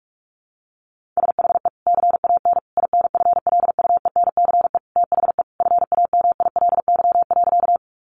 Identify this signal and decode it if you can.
Morse code, heard as a keyed tone, about 44 wpm, 700 Hz. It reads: SHE CAN INFLUENCE THE FAMILY.
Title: Sentences from Top 1100 Words 45wpm